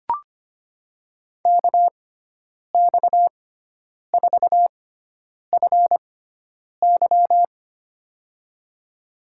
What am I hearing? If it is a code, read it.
Morse code, heard as a keyed tone, about 25 wpm, spaced as for 9 wpm, 700 Hz. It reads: KX4FY
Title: USA Call Signs - 5 Characters - 6x Character Spacing - 25wpm